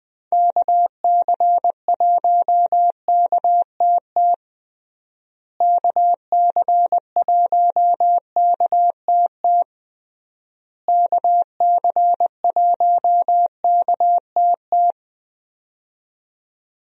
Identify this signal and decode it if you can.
Morse code, heard as a keyed tone, about 20 wpm, 700 Hz. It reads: KC1KTT KC1KTT KC1KTT
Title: USA Call Signs - 6 Characters - Repeated 3x - 20wpm